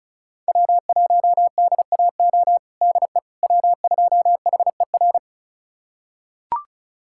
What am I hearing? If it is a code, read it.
Morse code, heard as a keyed tone, about 34 wpm, 700 Hz. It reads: W1DAO DE W2HER
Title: USA Callsign DE Callsign - 5 Characters - 35wpm